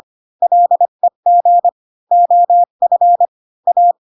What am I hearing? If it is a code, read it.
Morse code, heard as a keyed tone, about 25 wpm, 700 Hz. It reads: LEG OF A
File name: Pinocchio / Pinocchio ch1 20 25 ik1hge